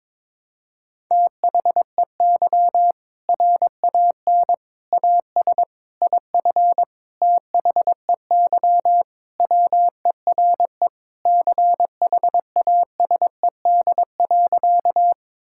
Morse code, heard as a keyed tone, about 22 wpm, 700 Hz. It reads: THEY RAN AS IF THEY WERE CHASED.
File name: practice / 22wpm / Sentences with 4-Word Phrases - 22wpm